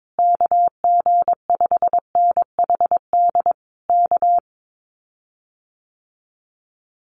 Morse code, heard as a keyed tone, about 22 wpm, 700 Hz. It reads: KG5NHD K
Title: Common QSO Phrases Encore 22wpm